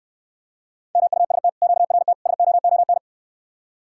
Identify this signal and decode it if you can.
Morse code, heard as a keyed tone, about 69 wpm, 700 Hz. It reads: MUST QRT SOON